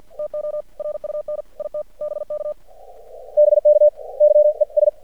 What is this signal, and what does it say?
Morse code, heard as a keyed tone, about 56 wpm, 600 Hz. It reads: TO RUN IT BK BK YES
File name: archived / CRSnet